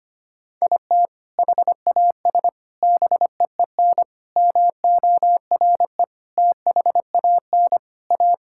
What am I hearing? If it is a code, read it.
Morse code, heard as a keyed tone, about 25 wpm, 700 Hz. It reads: IT HAS BEEN MORE THAN A
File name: practice / 25wpm / Sentences with 4-Word Phrases - 25wpm